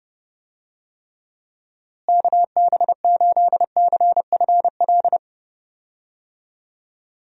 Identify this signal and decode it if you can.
Morse code, heard as a keyed tone, about 30 wpm, 700 Hz. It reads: KB8CFL
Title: USA Call Signs - 6 Characters - Repeated 2x - 30wpm